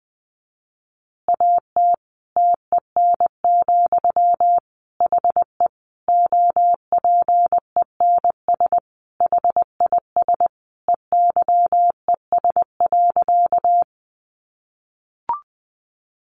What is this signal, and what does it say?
Morse code, heard as a keyed tone, about 20 wpm, 700 Hz. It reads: AT TEN, HE OPENS HIS EYES.
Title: Easy Phrases 20wpm